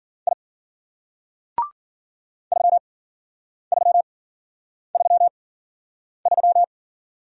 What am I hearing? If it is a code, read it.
Morse code, heard as a keyed tone, about 53 wpm, spaced as for 9 wpm, 700 Hz. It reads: I 3333